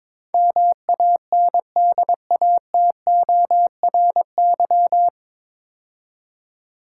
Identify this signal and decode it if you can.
Morse code, heard as a keyed tone, about 22 wpm, 700 Hz. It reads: MANDATORY
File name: 9 Letter Words - First 3 Letters Given - 22wpm